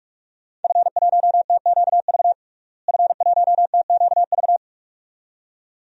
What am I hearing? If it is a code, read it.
Morse code, heard as a keyed tone, about 47 wpm, 700 Hz. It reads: F1TQV F1TQV
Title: DXCC-100 Call Signs - 5 Characters - Repeated 2x - 45wpm